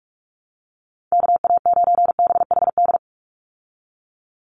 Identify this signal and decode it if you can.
Morse code, heard as a keyed tone, about 44 wpm, 700 Hz. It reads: KA9BHD
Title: Call Signs Repeated 2x - 45wpm